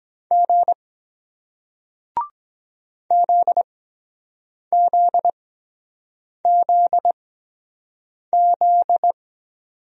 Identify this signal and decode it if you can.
Morse code, heard as a keyed tone, about 23 wpm, spaced as for 7 wpm, 700 Hz. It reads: G ZZZZ